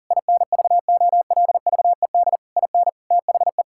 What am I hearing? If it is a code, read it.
Morse code, heard as a keyed tone, about 40 wpm, 700 Hz. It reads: INVOLVED IN THE
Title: Sentences from Top 1000 Words - Review Part 5 - 40wpm